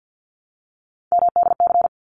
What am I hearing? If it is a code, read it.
Morse code, heard as a keyed tone, about 48 wpm, 700 Hz. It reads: MDC
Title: ARRL Sections 50wpm